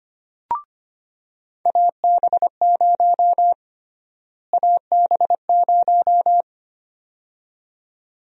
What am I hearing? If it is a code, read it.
Morse code, heard as a keyed tone, about 25 wpm, 700 Hz. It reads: AB0 AB0
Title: USA Call Sign Prefixes - Repeated 2x - 25wpm